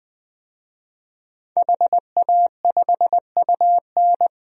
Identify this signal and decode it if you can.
Morse code, heard as a keyed tone, about 20 wpm, 700 Hz. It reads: HA5UN
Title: DXCC-100 Call Signs - 5 Characters - 20wpm